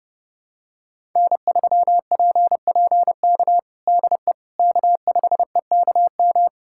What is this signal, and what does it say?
Morse code, heard as a keyed tone, about 30 wpm, 700 Hz. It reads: N3PPK DE K5EKM
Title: USA Callsign DE Callsign - 5 Characters - 30wpm